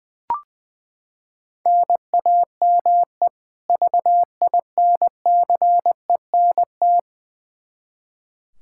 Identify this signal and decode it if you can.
Morse code, heard as a keyed tone, about 21 wpm, 700 Hz. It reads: NAME VINCENT
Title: Practice QSO - Exchange 1 - Beginning - 20wpm